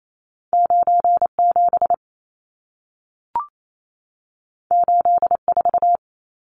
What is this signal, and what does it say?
Morse code, heard as a keyed tone, about 28 wpm, 700 Hz. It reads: 97 84
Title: Numbers - 2-Digit 28wpm